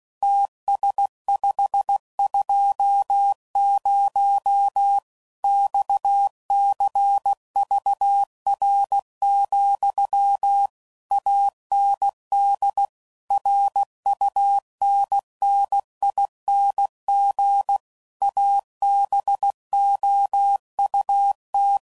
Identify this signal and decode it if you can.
Morse code, heard as a keyed tone, about 16 wpm, 800 Hz. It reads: TS520 XCVR, AND RUNNING ABOUT
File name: CW Code Groups @15WPM / QSOs / WD0FBT Spaced